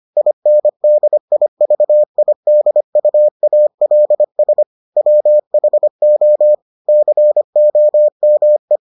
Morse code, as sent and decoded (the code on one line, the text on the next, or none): .. -. -.. .. ...- .. -.. ..- .- .-.. ... / .-- .... --- / -.-. --- -- .
INDIVIDUALS WHO COME